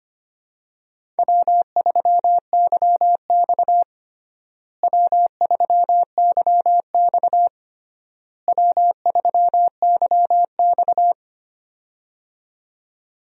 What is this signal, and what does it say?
Morse code, heard as a keyed tone, about 25 wpm, 700 Hz. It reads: W3YX W3YX W3YX
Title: USA Call Signs - 4 Characters - Repeated 3x - 25wpm